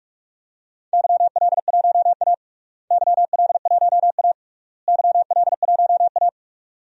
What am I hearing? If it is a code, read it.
Morse code, heard as a keyed tone, about 42 wpm, 700 Hz. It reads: YL1A YL1A YL1A
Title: DXCC-100 Call Signs - 4 Characters - Repeated 3x - 45wpm